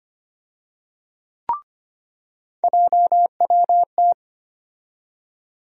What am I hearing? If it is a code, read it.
Morse code, heard as a keyed tone, about 25 wpm, 700 Hz. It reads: JWT
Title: USA Call Sign Postfixes - 3 Letters 25wpm